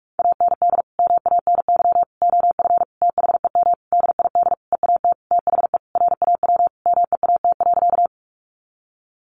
Morse code, heard as a keyed tone, about 44 wpm, 700 Hz. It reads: AND MANY OF THEM DID EAT THE RAW MEAT.
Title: Sentences from Top 1500 Words 45wpm